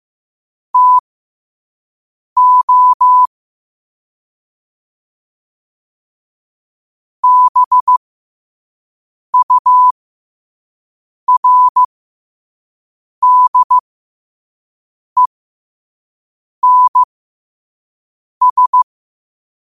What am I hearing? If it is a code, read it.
Morse code, heard as a keyed tone, about 15 wpm, spaced as for 5 wpm, 1000 Hz. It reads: TO BURDENS
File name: qotdfiles / 5wpm